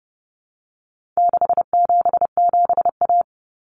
Morse code, heard as a keyed tone, about 30 wpm, 700 Hz. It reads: 677A
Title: Common Cut Numbers - 4-Digit 30wpm